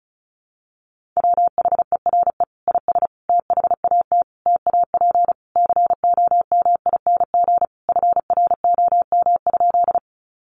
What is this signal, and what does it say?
Morse code, heard as a keyed tone, about 34 wpm, 700 Hz. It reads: WHERE IS THAT TAP COMING FROM?
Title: Sentences from Top 1400 Words 35wpm